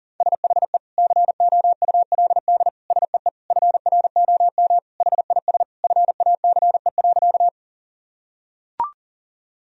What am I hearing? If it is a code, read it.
Morse code, heard as a keyed tone, about 40 wpm, 700 Hz. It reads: SHE COULD SEE FROM HIS FACE.
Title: Sentences from Top 1000 Words - Review Part 2 - 40wpm